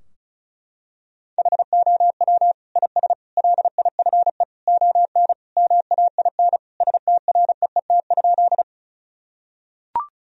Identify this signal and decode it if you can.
Morse code, heard as a keyed tone, about 36 wpm, 700 Hz. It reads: HOW IS LIFE ON MAIN STREET?
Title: Sentences from Top 1000 Words - Review Part 2 - 35wpm